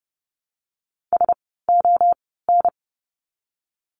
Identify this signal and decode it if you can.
Morse code, heard as a keyed tone, about 30 wpm, spaced as for 17 wpm, 700 Hz. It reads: SON
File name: Lesson 17f - Letter u - Words 30wpm